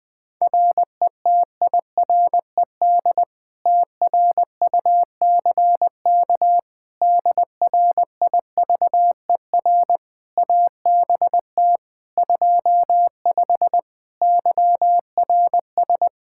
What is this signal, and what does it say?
Morse code, heard as a keyed tone, about 20 wpm, 700 Hz. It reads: RETIRED TRUCK DRIVER ABT 25 YRS
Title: Practice QSO - Exchange 3 - More Info - 20wpm